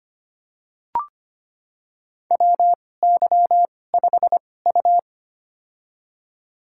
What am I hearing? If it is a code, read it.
Morse code, heard as a keyed tone, about 25 wpm, spaced as for 18 wpm, 700 Hz. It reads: WY5U
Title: USA Call Signs - 2x Character Spacing - 25wpm